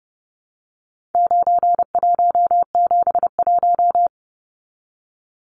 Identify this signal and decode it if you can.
Morse code, heard as a keyed tone, about 30 wpm, 700 Hz. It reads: 9171
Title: Numbers - 4-Digit 30wpm